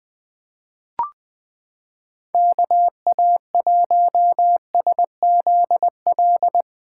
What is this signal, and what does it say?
Morse code, heard as a keyed tone, about 21 wpm, 700 Hz. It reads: KA1SZL
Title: USA Callsign DE Callsign - 6 Characters - 20wpm